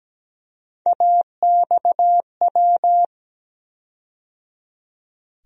Morse code, heard as a keyed tone, about 17 wpm, 700 Hz. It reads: AXW
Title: USA Call Sign Postfixes - 3 Letters 17wpm